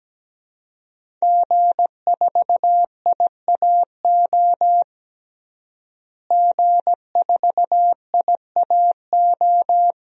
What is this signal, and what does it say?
Morse code, heard as a keyed tone, about 17 wpm, 700 Hz. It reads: G4IAO G4IAO
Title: DXCC-100 Call Signs - 5 Characters - Repeated 2x - 17wpm